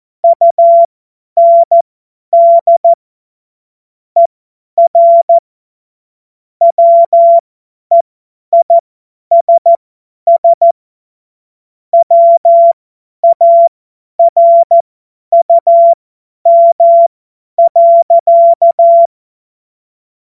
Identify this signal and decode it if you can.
Morse code, heard as a keyed tone, about 14 wpm, spaced as for 10 wpm, 680 Hz. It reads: UND ER WEISS WARUM.
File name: Morsekurs / Witze / Witz 31. Mai 2022 50 F70